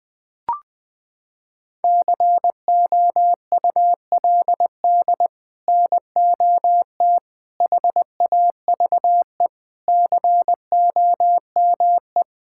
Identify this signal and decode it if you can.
Morse code, heard as a keyed tone, about 20 wpm, 700 Hz. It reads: COULD NOT HAVE COME
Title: Sets of 8 Words - 20wpm